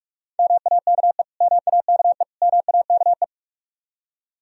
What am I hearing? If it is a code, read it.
Morse code, heard as a keyed tone, about 46 wpm, 700 Hz. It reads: MAKE MAKE MAKE
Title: Top 100 Words Repeated 45wpm